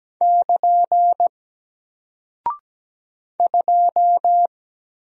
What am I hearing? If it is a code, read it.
Morse code, heard as a keyed tone, about 17 wpm, 700 Hz. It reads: <KN> 2
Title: Single Character (Letter:Number:Punctuation:Prosign) Alt 17wpm